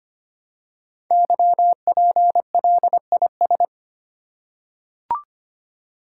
Morse code, heard as a keyed tone, about 25 wpm, 700 Hz. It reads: YPLIS
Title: Code Groups - 5 Letters - 25wpm